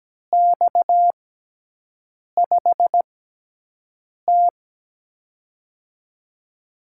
Morse code, heard as a keyed tone, about 17 wpm, spaced as for 6 wpm, 700 Hz. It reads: X5T
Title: USA Call Signs - 4 Characters - 6x Character Spacing - 17wpm